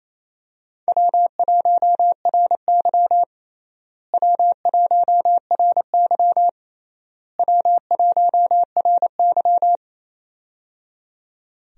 Morse code, heard as a keyed tone, about 28 wpm, 700 Hz. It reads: W1RY W1RY W1RY
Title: USA Call Signs - 4 Characters - Repeated 3x - 28wpm